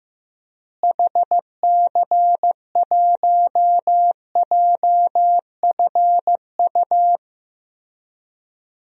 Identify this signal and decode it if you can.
Morse code, heard as a keyed tone, about 15 wpm, 700 Hz. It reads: HC1JFU
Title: DXCC-100 Call Signs - 6 Characters - Repeated 3x - 15wpm